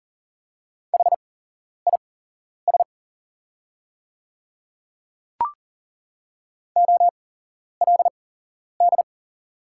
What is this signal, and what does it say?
Morse code, heard as a keyed tone, about 39 wpm, spaced as for 11 wpm, 700 Hz. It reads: HIS OLD
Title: 8x Character Spacing - Top 200 Words - 40wpm